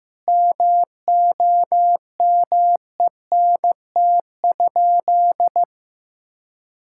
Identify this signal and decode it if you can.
Morse code, heard as a keyed tone, about 15 wpm, 700 Hz. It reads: MOMENT?
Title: Sentences from Top 1600 Words 15wpm